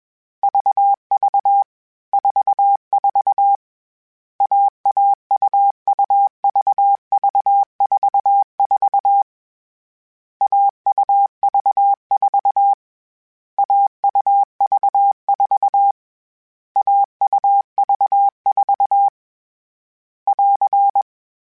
Morse code, heard as a keyed tone, about 21 wpm, 800 Hz. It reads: VV 44 AAUUVV44 AUV4 AUV4 AUV4 <AR>
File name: CW Code Groups-mp3s / Sets mp3s / AUV AUV4 sp @20